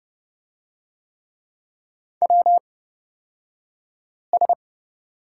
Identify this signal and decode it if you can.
Morse code, heard as a keyed tone, about 30 wpm, spaced as for 5 wpm, 700 Hz. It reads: WS
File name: Lesson 38d - Letter q - Two Characters 30wpm